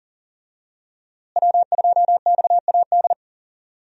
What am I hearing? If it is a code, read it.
Morse code, heard as a keyed tone, about 40 wpm, 700 Hz. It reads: W2XAD
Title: USA Call Signs - 5 Characters - Repeated 3x - 40wpm